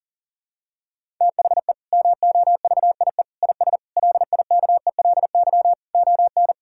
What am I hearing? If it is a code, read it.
Morse code, heard as a keyed tone, about 37 wpm, 700 Hz. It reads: THE MOVIE IS LIKELY ON
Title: Sentences from Top 1000 Words - Review Part 5 - 40wpm